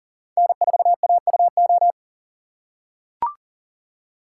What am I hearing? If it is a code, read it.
Morse code, heard as a keyed tone, about 40 wpm, 700 Hz. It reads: N4AUO
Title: USA Call Signs - 5 Characters - 6x Character Spacing - 40wpm